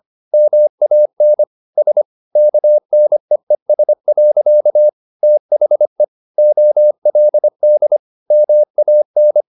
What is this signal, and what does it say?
Morse code, heard as a keyed tone, about 25 wpm, 600 Hz. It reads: MAN S KNEES. THE OLD MAN
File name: the old man and the sea part 1 25wpm